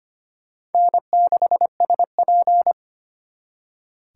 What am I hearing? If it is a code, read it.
Morse code, heard as a keyed tone, about 32 wpm, 700 Hz. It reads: N6SP